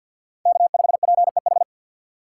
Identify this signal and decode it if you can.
Morse code, heard as a keyed tone, about 49 wpm, 700 Hz. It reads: K5PEH
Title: USA Call Signs - 5 Characters - 3x Character Spacing - 50wpm